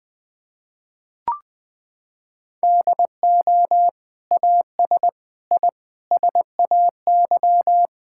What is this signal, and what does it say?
Morse code, heard as a keyed tone, about 21 wpm, 700 Hz. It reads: DO AS I SAY
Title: Sets of 8 Words - 20wpm